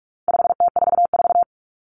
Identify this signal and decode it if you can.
Morse code, heard as a keyed tone, about 44 wpm, 700 Hz. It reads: 5T44
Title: Common Cut Numbers - 4-Digit 45wpm